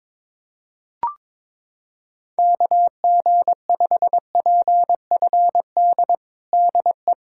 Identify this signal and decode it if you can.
Morse code, heard as a keyed tone, about 22 wpm, 700 Hz. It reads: KG5PFD DE
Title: USA Callsign DE Callsign 22wpm